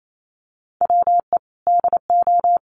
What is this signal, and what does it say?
Morse code, heard as a keyed tone, about 28 wpm, 700 Hz. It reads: WE DO
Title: Sets of 2 Words 28wpm